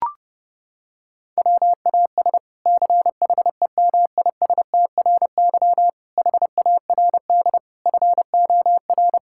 Morse code, heard as a keyed tone, about 28 wpm, 700 Hz. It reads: WAS CHEMISTRY HARD FOR
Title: Sentences from Top 1000 Words - Review Part 6 - 30wpm